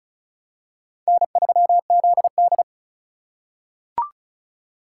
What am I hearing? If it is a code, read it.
Morse code, heard as a keyed tone, about 36 wpm, 700 Hz. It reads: N3ZD
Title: USA Call Signs - 4x Character Spacing - 35wpm